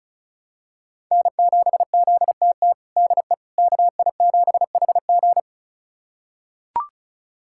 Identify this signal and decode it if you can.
Morse code, heard as a keyed tone, about 36 wpm, 700 Hz. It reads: N7ZTT DE KI7HG
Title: USA Callsign DE Callsign - 5 Characters - 35wpm